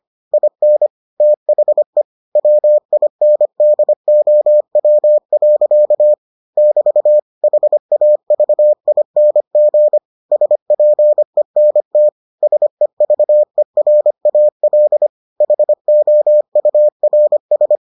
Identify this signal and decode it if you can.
Morse code, heard as a keyed tone, about 25 wpm, 600 Hz. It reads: IN THE WINDOW. <BT> HAVING SPENT SEVERAL HOURS